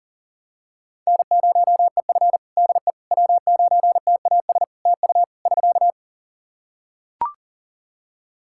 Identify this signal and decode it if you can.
Morse code, heard as a keyed tone, about 40 wpm, 700 Hz. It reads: N0EF DE W9TAS TU <SK>
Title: Practice QSO - Exchange 4 - Ending - 40wpm